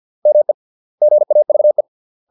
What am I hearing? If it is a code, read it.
Morse code, heard as a keyed tone, about 50 wpm, 600 Hz. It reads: ME GAVE